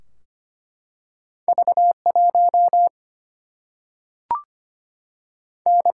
Morse code, heard as a keyed tone, about 25 wpm, 700 Hz. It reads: V 1 N
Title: Single-Letter Single-Number 25wpm